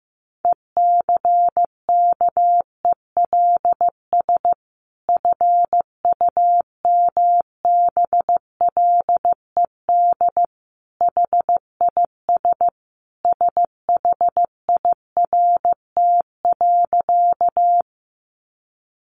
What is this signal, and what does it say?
Morse code, heard as a keyed tone, about 15 wpm, 700 Hz. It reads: ECKELS FUMBLED HIS SHIRT.
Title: A Sound Of Thunder 15wpm